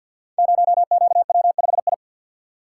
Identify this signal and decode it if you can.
Morse code, heard as a keyed tone, about 50 wpm, 700 Hz. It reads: 0QW5I
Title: Code Groups - 5 Letter:Numbers - 50wpm